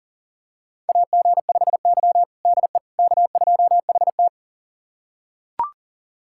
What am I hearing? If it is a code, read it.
Morse code, heard as a keyed tone, about 40 wpm, 700 Hz. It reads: AG5Y DE K2HT
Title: USA Callsign DE Callsign - 4 Characters - 40wpm